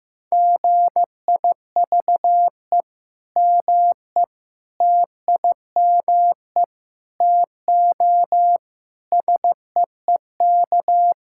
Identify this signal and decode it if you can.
Morse code, heard as a keyed tone, about 15 wpm, 700 Hz. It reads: GIVE ME TIME TO SEEK
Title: Sentences from Top 1600 Words 15wpm